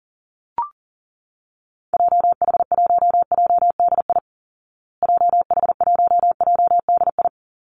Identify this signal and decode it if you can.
Morse code, heard as a keyed tone, about 40 wpm, 700 Hz. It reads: JH1JDI JH1JDI
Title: Call Signs Repeated 2x - 40wpm